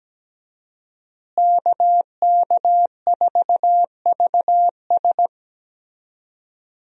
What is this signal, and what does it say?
Morse code, heard as a keyed tone, about 17 wpm, 700 Hz. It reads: KK4VS
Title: USA Call Signs - 5 Characters - Repeated 3x - 17wpm